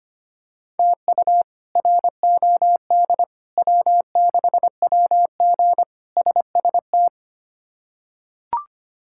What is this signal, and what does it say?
Morse code, heard as a keyed tone, about 25 wpm, 700 Hz. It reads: TU ROD W6WG SST
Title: K1USN SST QSO Practice - 25wpm